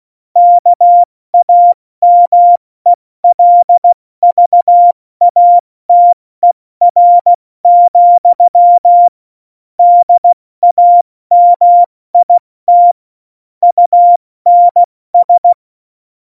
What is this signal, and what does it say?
Morse code, heard as a keyed tone, about 16 wpm, 700 Hz. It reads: KAMELVATER, DAMIT UNS
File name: Morsekurs / Witze / Witz 06. November 2018, 70 F80